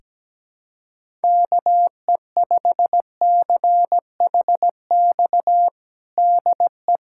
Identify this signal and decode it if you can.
Morse code, heard as a keyed tone, about 17 wpm, 700 Hz. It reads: KE5CHX DE
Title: USA Callsign DE Callsign - 6 Characters - 17wpm